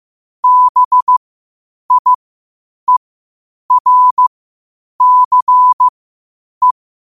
Morse code, heard as a keyed tone, about 15 wpm, spaced as for 8 wpm, 1000 Hz. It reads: BIERCE
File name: qotdfiles / 8wpm